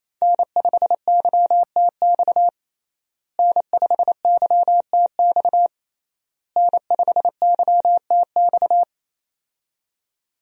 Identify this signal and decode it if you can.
Morse code, heard as a keyed tone, about 28 wpm, 700 Hz. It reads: N5YTX N5YTX N5YTX
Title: USA Call Signs - 5 Characters - Repeated 3x - 28wpm